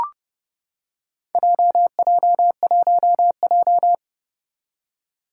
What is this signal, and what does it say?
Morse code, heard as a keyed tone, about 32 wpm, 700 Hz. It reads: JJ1J
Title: Four Challenging Letter-Number Combinations 30wpm